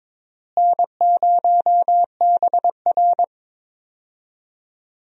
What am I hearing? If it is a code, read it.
Morse code, heard as a keyed tone, about 22 wpm, 700 Hz. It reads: N0BR
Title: USA Call Signs - 4 Characters - Repeated 3x - 22wpm